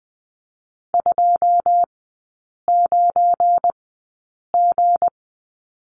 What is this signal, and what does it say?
Morse code, heard as a keyed tone, about 20 wpm, spaced as for 8 wpm, 700 Hz. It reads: 29G
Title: 178 Lesson 29e - Slash - Three Characters 20wpm